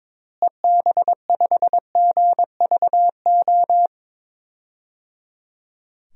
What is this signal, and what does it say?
Morse code, heard as a keyed tone, about 22 wpm, 700 Hz. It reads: EB5GVO
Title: DXCC-100 Call Signs - 6 Characters - Repeated 3x - 22wpm